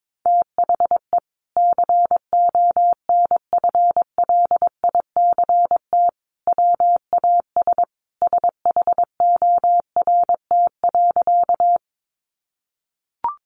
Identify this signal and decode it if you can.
Morse code, heard as a keyed tone, about 22 wpm, 700 Hz. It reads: THE CONFLICT WAS SHORT.
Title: Sentences from Top 1300 Words 22wpm